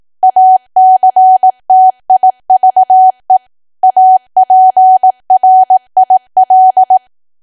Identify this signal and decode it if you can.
Morse code, heard as a keyed tone, about 18 wpm, 750 Hz. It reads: ACTIVE APRIL
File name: cwa / W1AW-Level2 / 18WPM